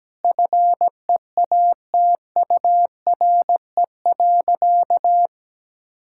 Morse code, heard as a keyed tone, about 17 wpm, 700 Hz. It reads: FEATURE.
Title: Sentences from Top 1000 Words - Review Part 5 - 17wpm